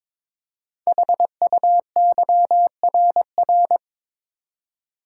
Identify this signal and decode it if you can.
Morse code, heard as a keyed tone, about 21 wpm, 700 Hz. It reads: HUYRR
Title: Code Groups - 5 Letters - 22wpm